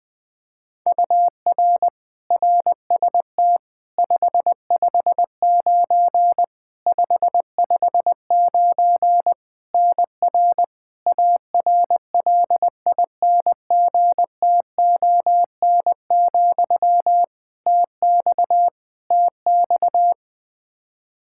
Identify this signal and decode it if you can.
Morse code, heard as a keyed tone, about 20 wpm, 700 Hz. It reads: UR RST 559 559 NR ARLINGTON, TX TX
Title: Practice QSO - Exchange 1 - Beginning - 20wpm